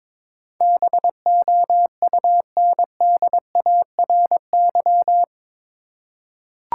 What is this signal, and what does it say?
Morse code, heard as a keyed tone, about 22 wpm, 700 Hz. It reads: BOUNDARY
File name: practice / 22wpm / Eight Letter Words - 22wpm